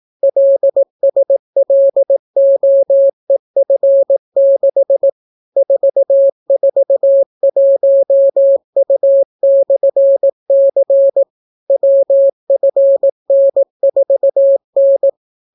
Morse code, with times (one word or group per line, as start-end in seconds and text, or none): 0.2-5.1 LSLOEF6
5.6-11.2 441U/C
11.7-15.1 WFN4N